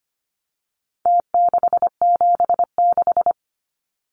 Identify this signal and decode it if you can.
Morse code, heard as a keyed tone, about 25 wpm, 700 Hz. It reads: T676
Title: Numbers - Contest Style - 4-Digit 25wpm